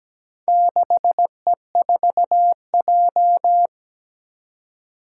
Morse code, heard as a keyed tone, about 17 wpm, 700 Hz. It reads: 6E4J
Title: Four Challenging Letter-Number Combinations 17wpm